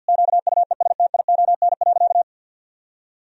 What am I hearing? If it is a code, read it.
Morse code, heard as a keyed tone, about 50 wpm, 700 Hz. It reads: QUESTION.
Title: Sentences from Top 1000 Words - Review Part 3 - 50wpm